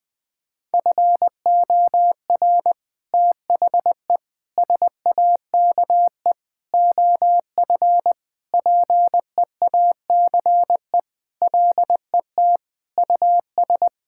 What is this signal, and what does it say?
Morse code, heard as a keyed tone, about 20 wpm, 700 Hz. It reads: FOR THE SAKE OF PEACE LET US
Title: Sentences with 4-Word Phrases - Spoken Beginning - 20wpm